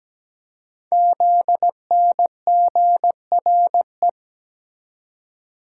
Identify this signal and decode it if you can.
Morse code, heard as a keyed tone, about 17 wpm, 700 Hz. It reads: ZNGRE
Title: Code Groups - 5 Letters - 17wpm